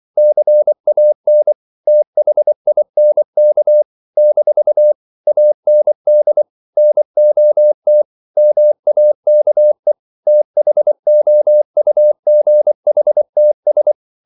Morse code, heard as a keyed tone, about 24 wpm, 600 Hz. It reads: CAN THINK - AND NOT MAKE THOUGHTS
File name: CW / Practice / If24